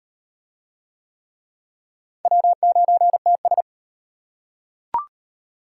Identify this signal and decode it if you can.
Morse code, heard as a keyed tone, about 38 wpm, 700 Hz. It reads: W9TS